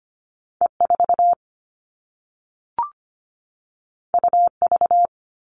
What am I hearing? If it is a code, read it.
Morse code, heard as a keyed tone, about 25 wpm, 700 Hz. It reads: E 4 U V
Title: All Cut Numbers - 2-Digit 25wpm